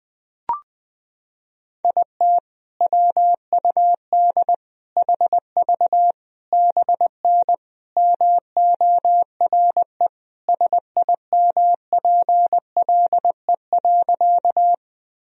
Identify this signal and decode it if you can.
Morse code, heard as a keyed tone, about 21 wpm, 700 Hz. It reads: IT WUD HV BN MORE SIMPLE.
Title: Sentences from Top 400 Words and Abbreviations 20wpm